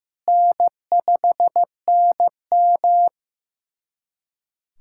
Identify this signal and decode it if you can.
Morse code, heard as a keyed tone, about 15 wpm, 700 Hz. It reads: N5NM
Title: USA Callsign DE Callsign - 4 Characters - 15wpm